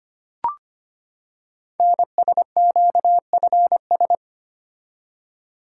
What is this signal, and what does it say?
Morse code, heard as a keyed tone, about 25 wpm, 700 Hz. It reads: NSQFS
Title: Code Groups - 5 Letters - 25wpm